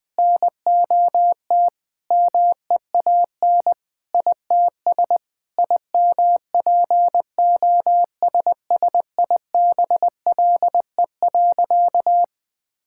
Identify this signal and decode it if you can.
Morse code, heard as a keyed tone, about 20 wpm, 700 Hz. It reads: NOT MEAN ITS IMPOSSIBLE.
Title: Sentences with 5-Word Phrases - Spoken Beginning - 20wpm